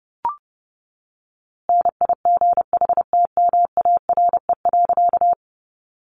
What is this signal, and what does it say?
Morse code, heard as a keyed tone, about 30 wpm, 700 Hz. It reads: NIGHTMARE.
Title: A Sound Of Thunder 30wpm